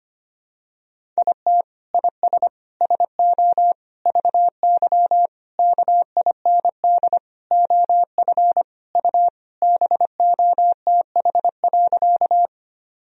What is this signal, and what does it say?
Morse code, heard as a keyed tone, about 25 wpm, 700 Hz. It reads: IT IS SO VY KIND OF U BOTH.
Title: Sentences from Top 300 Words and Abbreviations 25wpm